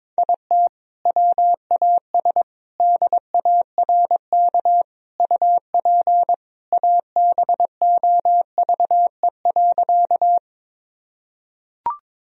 Morse code, heard as a keyed tone, about 22 wpm, 700 Hz. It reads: IT WAS DARK UP ABOVE.
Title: Sentences from Top 1000 Words - Review Part 2 - 22wpm